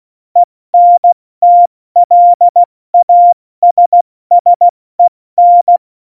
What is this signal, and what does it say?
Morse code, heard as a keyed tone, about 16 wpm, 700 Hz. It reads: ENTLASSEN
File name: Morsekurs / Witze / Witz 08. September 2020, 70 F80